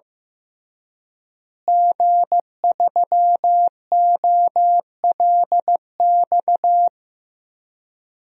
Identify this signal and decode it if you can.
Morse code, heard as a keyed tone, about 15 wpm, 700 Hz. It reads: G3OLX
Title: DXCC-100 Call Signs - 5 Characters - Repeated 3x - 15wpm